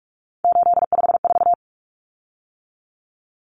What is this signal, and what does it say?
Morse code, heard as a keyed tone, about 44 wpm, 700 Hz. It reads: 854
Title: Common Cut Numbers - 3-Digit 45wpm